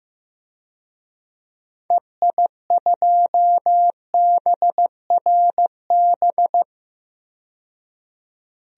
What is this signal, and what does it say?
Morse code, heard as a keyed tone, about 15 wpm, 700 Hz. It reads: EI2BRB
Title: DXCC-100 Call Signs - 6 Characters - 15wpm